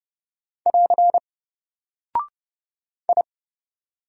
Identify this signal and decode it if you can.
Morse code, heard as a keyed tone, about 30 wpm, 700 Hz. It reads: <AR> I
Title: Single Character (Letter:Number:Punctuation:Prosign) Alt 30wpm